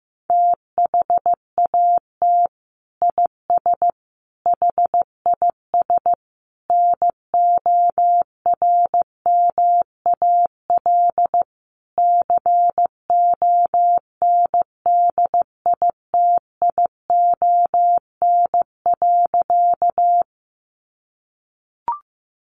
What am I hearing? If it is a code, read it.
Morse code, heard as a keyed tone, about 15 wpm, 700 Hz. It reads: THAT IS HIS NORMAL CONDITION.
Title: Sentences from Top 1200 Words 15wpm